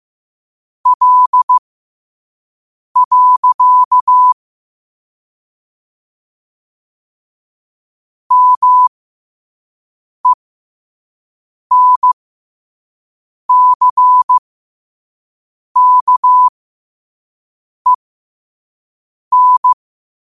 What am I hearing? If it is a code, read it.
Morse code, heard as a keyed tone, about 15 wpm, spaced as for 5 wpm, 1000 Hz. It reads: L. MENCKEN
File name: qotdfiles / 5wpm